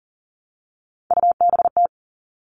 Morse code, heard as a keyed tone, about 40 wpm, 700 Hz. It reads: UBT
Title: All Cut Numbers - 3-Digit 40wpm